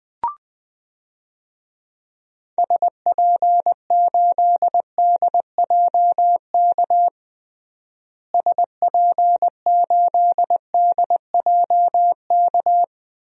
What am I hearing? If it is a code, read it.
Morse code, heard as a keyed tone, about 20 wpm, 700 Hz. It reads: SP8DJK SP8DJK
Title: DXCC-100 Call Signs - 6 Characters - Repeated 3x - 20wpm